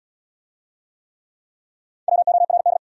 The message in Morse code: WORD

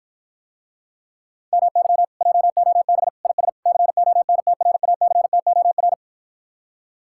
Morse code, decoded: MY JOB IS CONTRACTOR

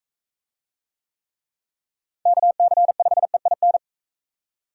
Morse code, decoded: KC5EIN

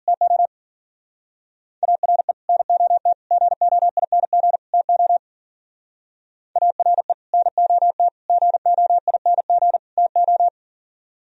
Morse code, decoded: TO ARE NOT GOING TO ARE NOT GOING TO

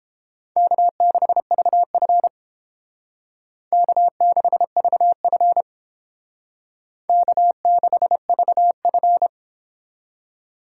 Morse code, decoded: K6VF K6VF K6VF

